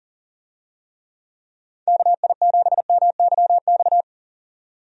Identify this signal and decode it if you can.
Morse code, heard as a keyed tone, about 40 wpm, 700 Hz. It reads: KI7MYX
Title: USA Call Signs - 6 Characters - 40wpm